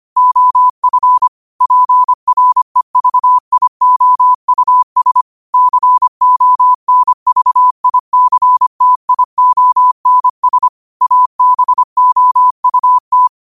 Morse code, heard as a keyed tone, about 25 wpm, 1000 Hz. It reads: OF PREVIOUS CONVICTIONS ABOUT
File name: qotdfiles / 25wpm